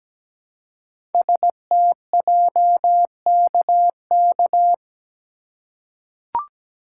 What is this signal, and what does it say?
Morse code, heard as a keyed tone, about 17 wpm, 700 Hz. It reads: STJKK
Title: Code Groups - 5 Letters - 17wpm